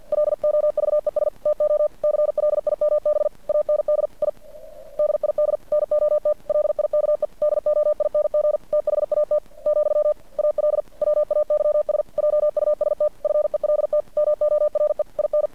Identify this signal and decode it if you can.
Morse code, heard as a keyed tone, about 50 wpm, 600 Hz. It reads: LOVES TO CLIMB AND I DID NOT LIKE DOING THAT , ALWAYS JUST FELT MORE IN
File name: archived / CRSnet